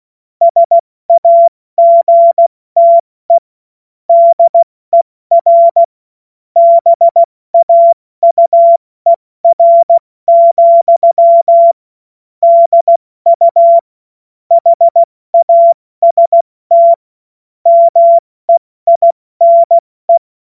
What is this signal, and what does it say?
Morse code, heard as a keyed tone, about 16 wpm, spaced as for 14 wpm, 680 Hz. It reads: SAGTE DER BAUER, DU HAST MEINE
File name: Morsekurs / Witze / Witz 07. März 2023 70 F80